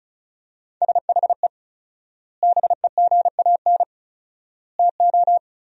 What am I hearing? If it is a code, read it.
Morse code, heard as a keyed tone, about 34 wpm, 700 Hz. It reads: SHE BEGAN TO
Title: 4x Word-Spacing - Sentences from Top 500 Words - 35wpm